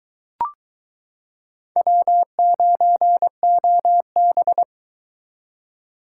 Morse code, heard as a keyed tone, about 22 wpm, 700 Hz. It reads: W9OB